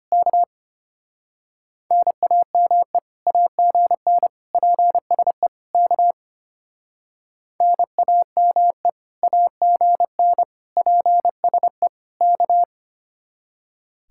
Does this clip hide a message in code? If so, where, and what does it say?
Yes: Morse code from 0.1 to 12.6 s, heard as a keyed tone, about 32 wpm, 700 Hz. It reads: K NAME AGN PSE K NAME AGN PSE K